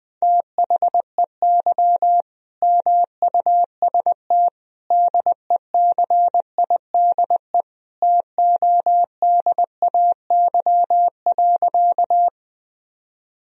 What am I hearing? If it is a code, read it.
Morse code, heard as a keyed tone, about 21 wpm, 700 Hz. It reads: THEY MUST DECIDE TODAY.
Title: Sentences from Top 1000 Words - Review Part 3 - 20wpm